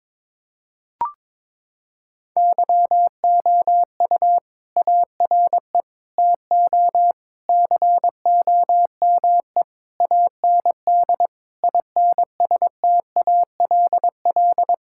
Morse code, heard as a keyed tone, about 22 wpm, 700 Hz. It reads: YOU ARE TO COME AND INSTALL
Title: Sentences from Top 1600 Words 22wpm